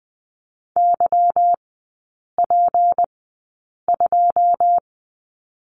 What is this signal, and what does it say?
Morse code, heard as a keyed tone, about 20 wpm, spaced as for 8 wpm, 700 Hz. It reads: YP2
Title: 152 Lesson 25e - Letter p - Three Characters 20wpm